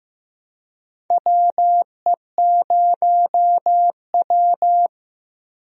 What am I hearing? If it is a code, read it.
Morse code, heard as a keyed tone, about 15 wpm, 700 Hz. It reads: WE0W
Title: USA Call Signs - 4 Characters - Repeated 3x - 15wpm